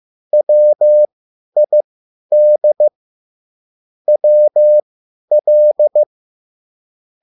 Morse code, heard as a keyed tone, about 15 wpm, spaced as for 10 wpm, 600 Hz. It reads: WID WL